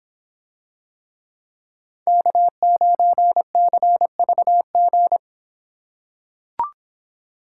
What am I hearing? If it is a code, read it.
Morse code, heard as a keyed tone, about 24 wpm, 700 Hz. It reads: K9CVG